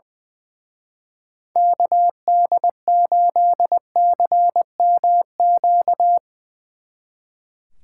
Morse code, heard as a keyed tone, about 21 wpm, 700 Hz. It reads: KD8CMQ
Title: Speed Racing - USA 2x3 Call Signs 20wpm